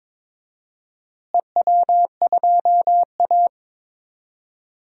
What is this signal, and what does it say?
Morse code, heard as a keyed tone, about 22 wpm, 700 Hz. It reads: EW2A
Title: DXCC-100 Call Signs - 4 Characters - Repeated 2x - 22wpm